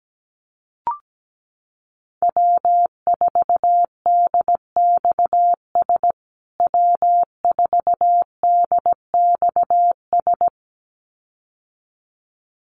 Morse code, heard as a keyed tone, about 17 wpm, 700 Hz. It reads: W4DXS W4DXS
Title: USA Call Signs Repeated 2x 17wpm